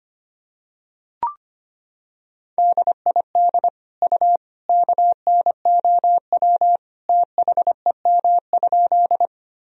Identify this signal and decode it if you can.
Morse code, heard as a keyed tone, about 25 wpm, 700 Hz. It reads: DID U KNOW THEM?
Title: Sentences from Top 100 Words and Abbreviations 25wpm